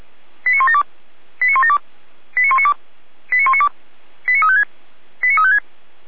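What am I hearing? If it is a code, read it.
Five-tone selective call (ZVEI), heard as quick runs of five tones, tones of about 70 ms. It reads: 89172 89172 89192 89192 89356 89356
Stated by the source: Five-tone ZVEI selective call received on an SDR.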